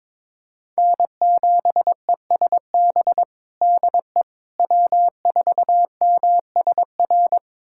Morse code, heard as a keyed tone, about 22 wpm, 700 Hz. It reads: N7ESB DE W4MSR
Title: USA Callsign DE Callsign - 5 Characters - 22wpm